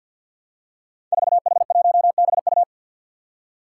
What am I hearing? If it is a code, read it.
Morse code, heard as a keyed tone, about 47 wpm, 700 Hz. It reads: 4H1BU
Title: Five Challenging Letter-Number Combinations 50wpm